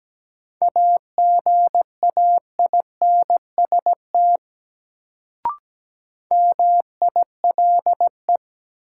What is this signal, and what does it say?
Morse code, heard as a keyed tone, about 17 wpm, 700 Hz. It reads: AGAINST MILE
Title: Sentences from Top 400 Words - New Word Drill - 17wpm